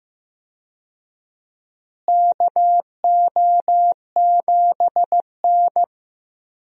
Morse code, heard as a keyed tone, about 15 wpm, 700 Hz. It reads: KO7N
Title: USA Call Signs - 4 Characters - Repeated 3x - 15wpm